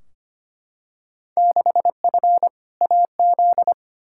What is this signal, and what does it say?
Morse code, heard as a keyed tone, about 25 wpm, 700 Hz. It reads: 6F AZ
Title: Field Day Exchanges Encore - 25wpm